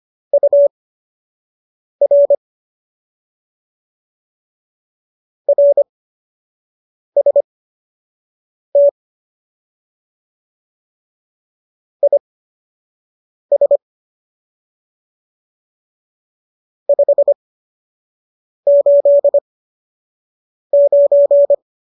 Morse code, heard as a keyed tone, about 25 wpm, spaced as for 6 wpm, 570 Hz. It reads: UR RST IS 589
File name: qso003 06